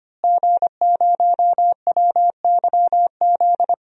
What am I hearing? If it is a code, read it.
Morse code, heard as a keyed tone, about 25 wpm, 700 Hz. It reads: G0WYZ
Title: DXCC-100 Call Signs - 5 Characters - Repeated 3x - 25wpm